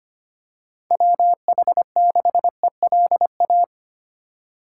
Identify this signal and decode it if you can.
Morse code, heard as a keyed tone, about 25 wpm, 700 Hz. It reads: WH6ELA
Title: USA Call Signs - 6 Characters - 2x Character Spacing - 25wpm